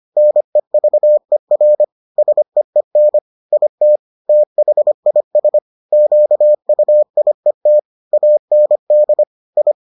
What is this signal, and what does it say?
Morse code, heard as a keyed tone, about 25 wpm, 600 Hz. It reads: NEVER SEEN IT THIS QUIET AND I